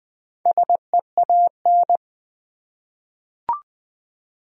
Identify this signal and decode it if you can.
Morse code, heard as a keyed tone, about 20 wpm, 700 Hz. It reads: SEAN
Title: USA Common Names - Max 5 Letters - 20wpm